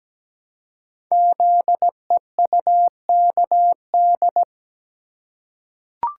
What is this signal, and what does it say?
Morse code, heard as a keyed tone, about 17 wpm, 700 Hz. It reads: ZEUKD
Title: Code Groups - 5 Letters - 17wpm